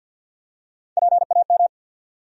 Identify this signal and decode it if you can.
Morse code, heard as a keyed tone, about 53 wpm, 700 Hz. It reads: PAM
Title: USA Common Names - Max 4 Letters - 50wpm